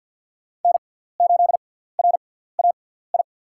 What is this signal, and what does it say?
Morse code, heard as a keyed tone, about 52 wpm, spaced as for 17 wpm, 700 Hz. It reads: N8RAI
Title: USA Call Signs - 5 Characters - 6x Character Spacing - 50wpm